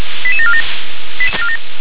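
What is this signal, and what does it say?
Five-tone selective call (ZVEI), heard as quick runs of five tones, tones of about 70 ms. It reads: 88747
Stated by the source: ZVEI five-tone selective call received by SDR.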